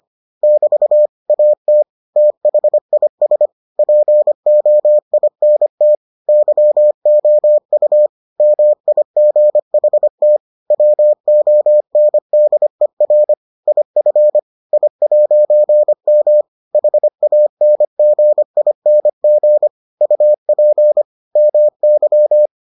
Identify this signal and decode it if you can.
Morse code, heard as a keyed tone, about 25 wpm, 600 Hz. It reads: <BT> AT THIS POINT YOU MIGHT WONDER IF I'M HANGING UP MY